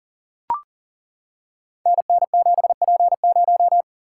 Morse code, heard as a keyed tone, about 38 wpm, 700 Hz. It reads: NN7P0
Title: Code Groups - 5 Letter:Numbers - 40wpm